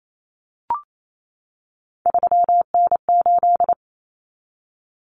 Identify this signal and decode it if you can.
Morse code, heard as a keyed tone, about 28 wpm, 700 Hz. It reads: 3N8
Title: Common Cut Numbers - 3-Digit 28wpm